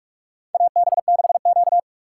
Morse code, heard as a keyed tone, about 46 wpm, 700 Hz. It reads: AB6Q
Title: USA Call Signs - 4 Characters - 3x Character Spacing - 45wpm